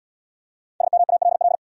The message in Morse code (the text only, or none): SMALL